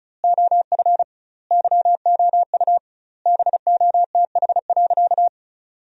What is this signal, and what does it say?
Morse code, heard as a keyed tone, about 36 wpm, 700 Hz. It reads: OF YOU BOTH.
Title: 2x Word-Spacing - Sentences from Top 300 Words - 35wpm